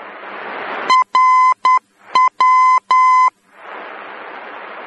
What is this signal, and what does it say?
Morse code, heard as a keyed tone, about 10 wpm, 1020 Hz. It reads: RW
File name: germany / mx / RW-D